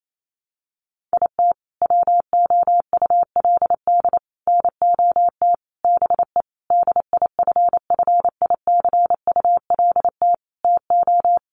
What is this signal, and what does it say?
Morse code, heard as a keyed tone, about 28 wpm, 700 Hz. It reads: IT WOULD NOT BE DIFFICULT TO
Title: Sentences from Top 600 Words 28wpm